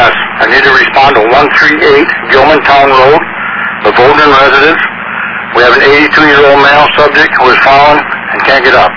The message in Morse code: none

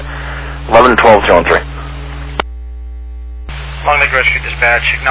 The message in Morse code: none

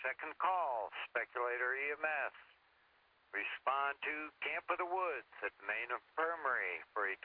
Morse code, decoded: none

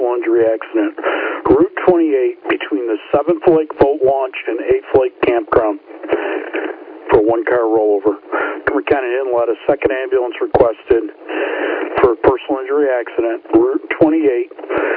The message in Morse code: none

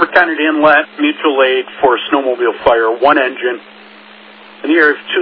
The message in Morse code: none

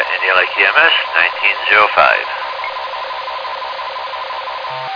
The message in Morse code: none

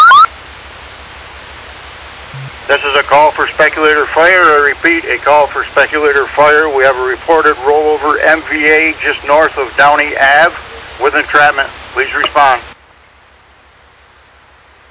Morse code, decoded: none

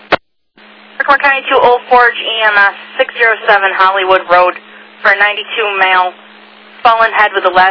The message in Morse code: none

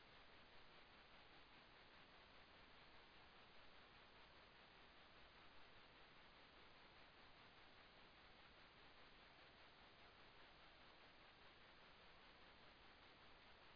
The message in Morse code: none

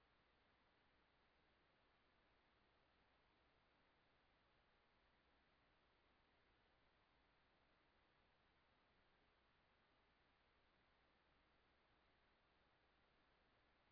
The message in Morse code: none

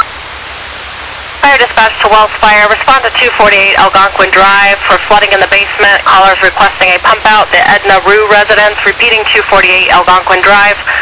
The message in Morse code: none

